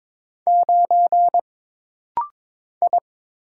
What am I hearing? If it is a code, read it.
Morse code, heard as a keyed tone, about 22 wpm, 700 Hz. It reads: 9 I
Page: This rapid-fire timing is geared to push your ICR (Instant Character Recognition) skills.